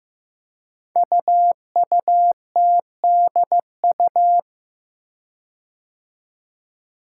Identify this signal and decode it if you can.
Morse code, heard as a keyed tone, about 15 wpm, 700 Hz. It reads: UUTDU
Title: Code Groups - 5 Letters - 15wpm